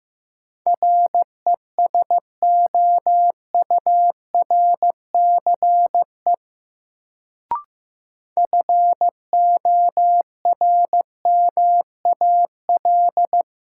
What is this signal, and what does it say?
Morse code, heard as a keyed tone, about 15 wpm, 700 Hz. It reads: RESOURCE FORMAL
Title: Sentences from Top 1300 Words - New Word Drill Alt - 15wpm